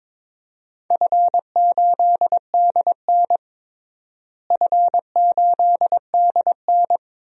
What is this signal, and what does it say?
Morse code, heard as a keyed tone, about 22 wpm, 700 Hz. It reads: F8DN F8DN
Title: DXCC-100 Call Signs - 4 Characters - Repeated 2x - 22wpm